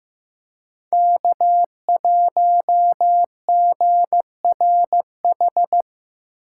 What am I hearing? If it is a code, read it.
Morse code, heard as a keyed tone, about 15 wpm, 700 Hz. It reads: K1GRH
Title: USA Call Signs - 6x Character Spacing - 15wpm